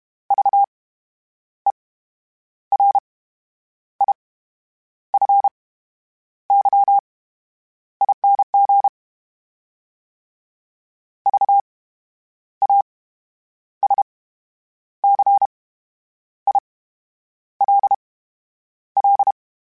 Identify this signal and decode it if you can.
Morse code, heard as a keyed tone, about 32 wpm, spaced as for 9 wpm, 800 Hz. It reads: VERIFYING VASCILL
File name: Special INGs / Special INGs-mp3s / VerySlowToFaster Ing @30